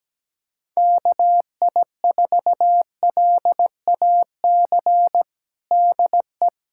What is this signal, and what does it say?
Morse code, heard as a keyed tone, about 17 wpm, 700 Hz. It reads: KI4LAC DE
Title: USA Callsign DE Callsign - 6 Characters - 17wpm